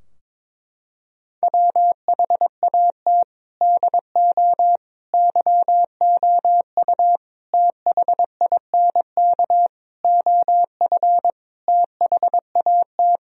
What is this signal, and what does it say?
Morse code, heard as a keyed tone, about 22 wpm, 700 Hz. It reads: WHAT DO YOU THINK OF THAT
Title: Sentences from Top 1000 Words - Review Part 2 - 22wpm